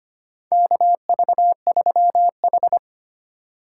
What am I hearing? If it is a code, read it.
Morse code, heard as a keyed tone, about 25 wpm, 700 Hz. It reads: KV3H
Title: USA Call Signs - 4 Characters - 3x Character Spacing - 25wpm